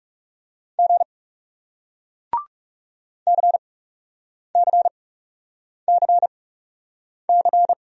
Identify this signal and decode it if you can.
Morse code, heard as a keyed tone, about 40 wpm, spaced as for 8 wpm, 700 Hz. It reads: G CCCC